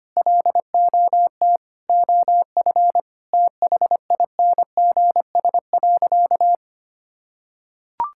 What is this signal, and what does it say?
Morse code, heard as a keyed tone, about 25 wpm, 700 Hz. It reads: LOT OF THINGS.
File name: practice / 25wpm / Sentences with 4-Word Phrases - 25wpm